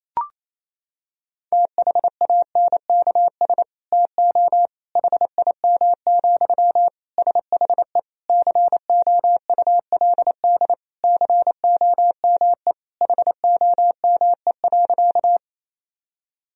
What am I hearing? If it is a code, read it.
Morse code, heard as a keyed tone, about 28 wpm, 700 Hz. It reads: THANKS TO HIM, SHE COULD COME HOME.
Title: Sentences from Top 1000 Words - Review Part 3 - 28wpm